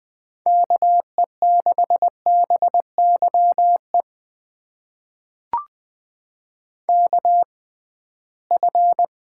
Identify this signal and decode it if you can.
Morse code, heard as a keyed tone, about 20 wpm, spaced as for 15 wpm, 700 Hz. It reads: KE6BYE KF
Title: USA Call Signs - 6 Characters - 6x Character Spacing - 20wpm